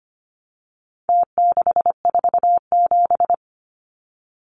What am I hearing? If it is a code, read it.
Morse code, heard as a keyed tone, about 25 wpm, 700 Hz. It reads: T647
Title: Numbers - Contest Style - 4-Digit 25wpm